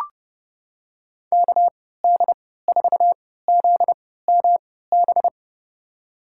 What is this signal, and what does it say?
Morse code, heard as a keyed tone, about 30 wpm, spaced as for 17 wpm, 700 Hz. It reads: KD4ZMB
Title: USA Call Signs - 6 Characters - 3x Character Spacing - 30wpm